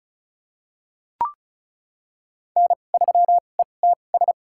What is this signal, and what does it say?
Morse code, heard as a keyed tone, about 36 wpm, spaced as for 25 wpm, 700 Hz. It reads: N3ETS
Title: USA Call Signs - 5 Characters - 2x Character Spacing - 35wpm